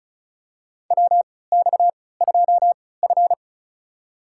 Morse code, heard as a keyed tone, about 34 wpm, spaced as for 20 wpm, 700 Hz. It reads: WX2F
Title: USA Call Signs - 4 Characters - 3x Character Spacing - 35wpm